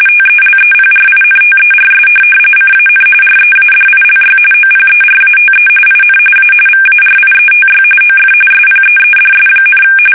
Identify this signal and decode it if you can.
RTTY, about 45.45 baud. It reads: TELEPRINTERS CAN READILY OPERATE IN A VARIETY OF SPEEDS,
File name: fsk850 / teletype-RTTY